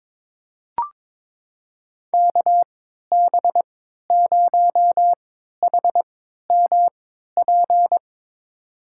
Morse code, heard as a keyed tone, about 21 wpm, spaced as for 12 wpm, 700 Hz. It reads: KB0HMP
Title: USA Call Signs - 3x Character Spacing - 22wpm